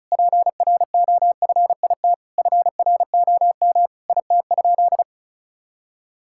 Morse code, heard as a keyed tone, about 34 wpm, 700 Hz. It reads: PROFIT FROM IT?
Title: Sentences from Top 1000 Words - Review Part 5 - 35wpm